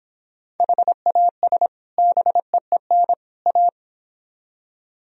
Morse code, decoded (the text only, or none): HAS BEEN A